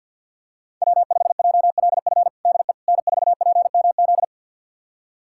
Morse code, heard as a keyed tone, about 49 wpm, 700 Hz. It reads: W5JLF DE N4PMZ
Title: USA Callsign DE Callsign - 5 Characters - 50wpm